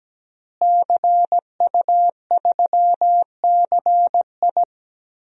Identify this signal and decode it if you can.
Morse code, heard as a keyed tone, about 17 wpm, 700 Hz. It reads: CU3CI
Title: DXCC-100 Call Signs - 5 Characters - Repeated 2x - 17wpm